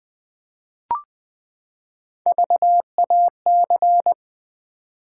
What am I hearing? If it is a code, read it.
Morse code, heard as a keyed tone, about 20 wpm, 700 Hz. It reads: VAC
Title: Build A Word - 3 Letters At A Time - 8-Letter Words - 20wpm